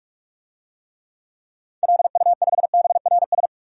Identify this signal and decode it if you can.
Morse code, heard as a keyed tone, about 46 wpm, 700 Hz. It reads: LU5BRS
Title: DXCC-100 Call Signs - 6 Characters - Repeated 3x - 45wpm